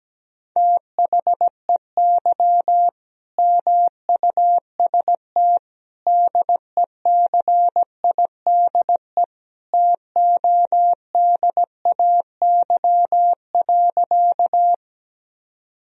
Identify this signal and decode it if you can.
Morse code, heard as a keyed tone, about 17 wpm, 700 Hz. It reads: THEY MUST DECIDE TODAY.
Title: Sentences from Top 1000 Words - Review Part 3 - 17wpm